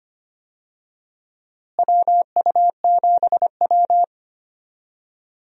Morse code, heard as a keyed tone, about 25 wpm, 700 Hz. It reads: WU7W
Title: USA Call Signs - 4 Characters - Repeated 2x - 25wpm